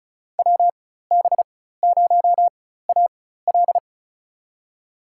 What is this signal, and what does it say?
Morse code, heard as a keyed tone, about 34 wpm, spaced as for 16 wpm, 700 Hz. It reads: WB0AL
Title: USA Call Signs - 5 Characters - 4x Character Spacing - 35wpm